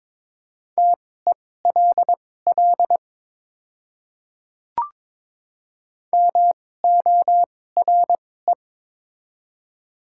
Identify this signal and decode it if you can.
Morse code, heard as a keyed tone, about 22 wpm, spaced as for 16 wpm, 700 Hz. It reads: TELL MORE
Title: 2x Character Spacing - Top 200 Words - 22wpm